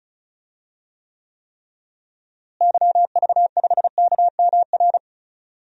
Morse code, heard as a keyed tone, about 34 wpm, 700 Hz. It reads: YV5KMR
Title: DXCC-100 Call Signs - 6 Characters - Repeated 3x - 35wpm